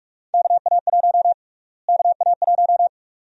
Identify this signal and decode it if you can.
Morse code, heard as a keyed tone, about 46 wpm, 700 Hz. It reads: KA1 KA1
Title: USA Call Sign Prefixes - Repeated 2x - 45wpm